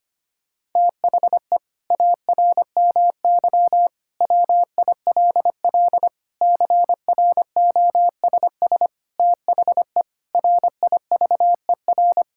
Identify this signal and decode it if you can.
Morse code, heard as a keyed tone, about 25 wpm, 700 Hz. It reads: THE ARMY WILL CROSS THE RIVER
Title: Sentences from Top 1000 Words - Review Part 5 - 25wpm